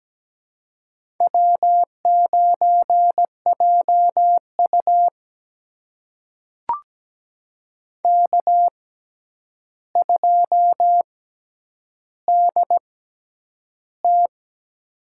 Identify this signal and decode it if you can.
Morse code, heard as a keyed tone, about 17 wpm, spaced as for 9 wpm, 700 Hz. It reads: W9JU K2DT
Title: USA Call Signs - 4 Characters - 6x Character Spacing - 17wpm